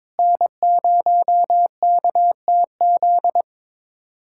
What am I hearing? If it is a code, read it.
Morse code, heard as a keyed tone, about 22 wpm, 700 Hz. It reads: N0KTZ
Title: USA Callsign DE Callsign - 5 Characters - 22wpm